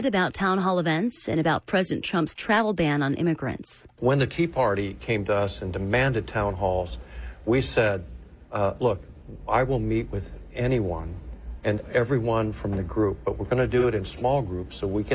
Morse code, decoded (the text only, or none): none